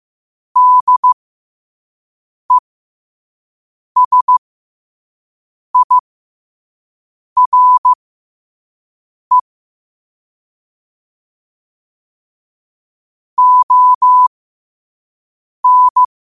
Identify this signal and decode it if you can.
Morse code, heard as a keyed tone, about 15 wpm, spaced as for 5 wpm, 1000 Hz. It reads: DESIRE ON
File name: qotdfiles / 5wpm